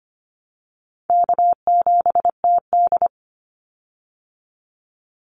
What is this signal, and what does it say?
Morse code, heard as a keyed tone, about 25 wpm, 700 Hz. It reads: K7TD
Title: Easy Call Signs Repeated 3x - 25wpm